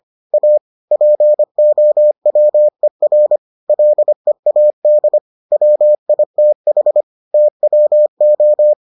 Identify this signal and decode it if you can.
Morse code, heard as a keyed tone, about 25 wpm, 600 Hz. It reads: A POWER LEAD WITH TWO